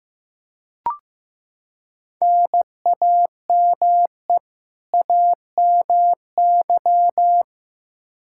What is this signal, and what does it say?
Morse code, heard as a keyed tone, about 15 wpm, 700 Hz. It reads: NAME AMY
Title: Practice QSO - Exchange 1 - Beginning - 15wpm